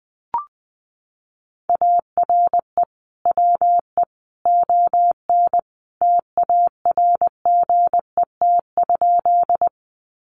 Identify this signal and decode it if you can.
Morse code, heard as a keyed tone, about 20 wpm, 700 Hz. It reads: ARE WE ON TARGET?
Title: Sentences from Top 1200 Words 20wpm